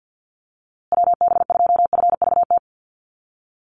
Morse code, heard as a keyed tone, about 50 wpm, 700 Hz. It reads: WB2FVT
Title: USA Call Signs Repeated 3x 50wpm